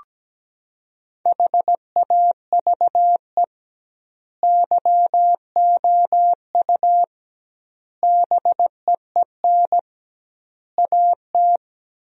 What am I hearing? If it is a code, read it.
Morse code, heard as a keyed tone, about 17 wpm, 700 Hz. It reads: HAVE YOU BEEN AT
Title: 2x Word-Spacing - Sentences from Top 300 Words - 17wpm